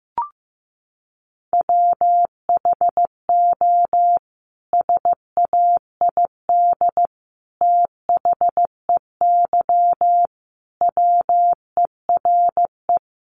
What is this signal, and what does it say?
Morse code, heard as a keyed tone, about 15 wpm, 700 Hz. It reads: WHO SAID THEY WERE
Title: Sentences from Top 700 Words 15wpm